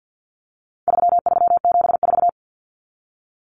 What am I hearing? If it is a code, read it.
Morse code, heard as a keyed tone, about 49 wpm, 700 Hz. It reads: 3374
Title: Numbers - 4-Digit 50wpm